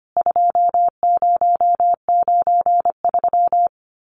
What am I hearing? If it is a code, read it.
Morse code, heard as a keyed tone, about 25 wpm, 700 Hz. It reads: 2093
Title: CWT Exchange 25wpm